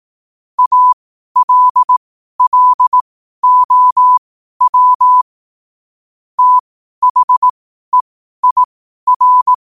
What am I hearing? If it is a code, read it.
Morse code, heard as a keyed tone, about 18 wpm, spaced as for 12 wpm, 1000 Hz. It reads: ALLOW THEIR
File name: qotdfiles / 12wpm